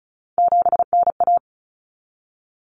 Morse code, heard as a keyed tone, about 34 wpm, 700 Hz. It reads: 7NA
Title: Common Cut Numbers - 3-Digit 35wpm